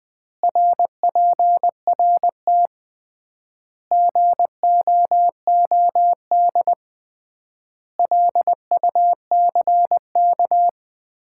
Morse code, heard as a keyed tone, about 21 wpm, 700 Hz. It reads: RPRT GOOD LUCK
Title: Common QSO Phrases - 3x Word-Spacing - 20wpm